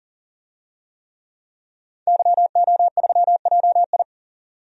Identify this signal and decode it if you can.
Morse code, heard as a keyed tone, about 40 wpm, 700 Hz. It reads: YO3JI
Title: DXCC-100 Call Signs - 5 Characters - 40wpm